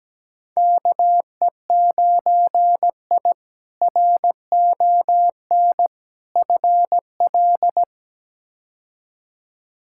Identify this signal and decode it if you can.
Morse code, heard as a keyed tone, about 17 wpm, 700 Hz. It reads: KE9I RON FL
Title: K1USN SST QSO Practice - 17wpm